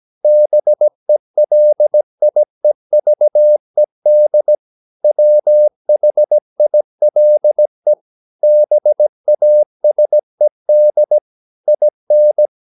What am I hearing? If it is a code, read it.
Morse code, heard as a keyed tone, about 17 wpm, 600 Hz. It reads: BELIEVED WHILE BASED IN
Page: Romanen Never af Ken Follet på engelsk.